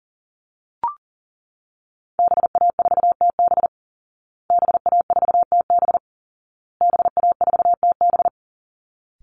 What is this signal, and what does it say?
Morse code, heard as a keyed tone, about 37 wpm, 700 Hz. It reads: BA4TB BA4TB BA4TB
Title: Call Signs Repeated 2x - 40wpm